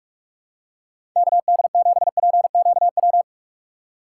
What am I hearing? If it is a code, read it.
Morse code, heard as a keyed tone, about 47 wpm, 700 Hz. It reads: KD7PQW
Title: USA Call Signs - 6 Characters - 3x Character Spacing - 45wpm